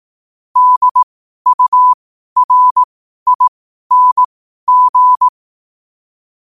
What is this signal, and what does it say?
Morse code, heard as a keyed tone, about 18 wpm, spaced as for 12 wpm, 1000 Hz. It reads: DURING
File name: qotdfiles / 12wpm